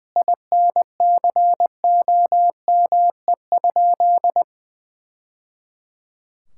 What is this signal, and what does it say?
Morse code, heard as a keyed tone, about 20 wpm, 700 Hz. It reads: INCOME?
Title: Sentences from Top 1000 Words - Review Part 5 - 20wpm